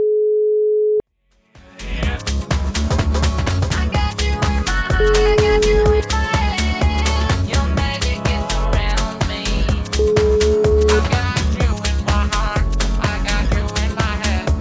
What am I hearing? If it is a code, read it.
Ringback tone (Europe, 425 Hz), with noise.